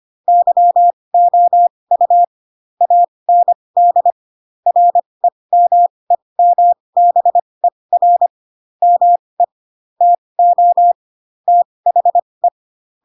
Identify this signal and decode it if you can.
Morse code, heard as a keyed tone, about 25 wpm, spaced as for 20 wpm, 700 Hz. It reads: YOU AND REMEMBER ME TO THE